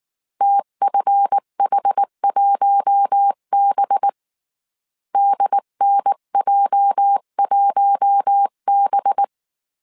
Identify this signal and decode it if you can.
Morse code, heard as a keyed tone, about 19 wpm, 800 Hz. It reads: TFH1B DNJ1B